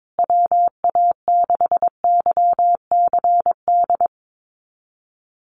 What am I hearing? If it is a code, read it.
Morse code, heard as a keyed tone, about 22 wpm, 700 Hz. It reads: WA6YCD
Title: USA Call Signs Repeated 2x 22wpm